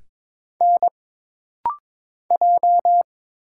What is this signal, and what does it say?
Morse code, heard as a keyed tone, about 22 wpm, 700 Hz. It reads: N J
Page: This rapid-fire timing is geared to push your ICR (Instant Character Recognition) skills.